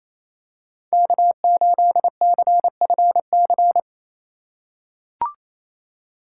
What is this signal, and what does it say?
Morse code, heard as a keyed tone, about 27 wpm, 700 Hz. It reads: K8CFC
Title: USA Call Signs - 5 Characters - 2x Character Spacing - 28wpm